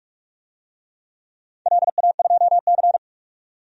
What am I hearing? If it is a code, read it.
Morse code, heard as a keyed tone, about 46 wpm, 700 Hz. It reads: LA2C
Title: DXCC-100 Call Signs - 4 Characters - 45wpm